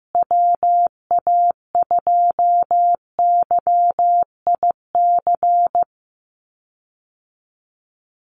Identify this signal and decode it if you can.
Morse code, heard as a keyed tone, about 15 wpm, 700 Hz. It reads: WA2YIC
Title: USA Call Signs Repeated 3x 15wpm